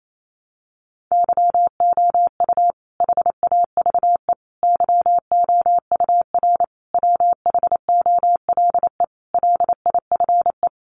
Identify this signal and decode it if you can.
Morse code, heard as a keyed tone, about 28 wpm, 700 Hz. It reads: YOU HAVE YOUR WHOLE LIFE
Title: Sentences from Top 400 Words 28wpm